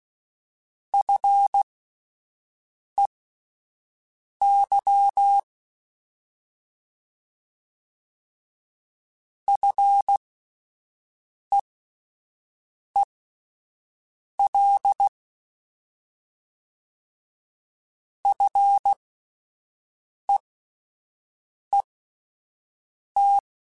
Morse code, heard as a keyed tone, about 16 wpm, spaced as for 5 wpm, 800 Hz. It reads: FEY FEEL FEET
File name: Alphabetical / Alphabetical mp3s / Fs @5